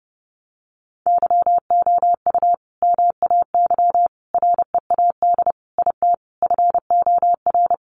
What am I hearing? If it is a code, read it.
Morse code, heard as a keyed tone, about 28 wpm, 700 Hz. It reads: YOU MAY READ IT FOR
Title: Sentences from Top 600 Words 30wpm